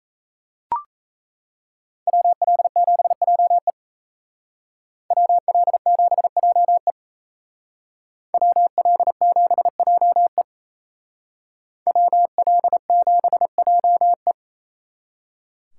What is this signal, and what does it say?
Morse code, heard as a keyed tone, about 35 wpm, 700 Hz. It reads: WL7JE WL7JE WL7JE WL7JE